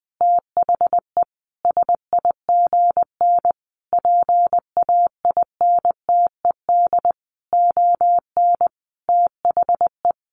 Morse code, heard as a keyed tone, about 19 wpm, 700 Hz. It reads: THE SIGN PAINTED ON THE
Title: A Sound Of Thunder 20wpm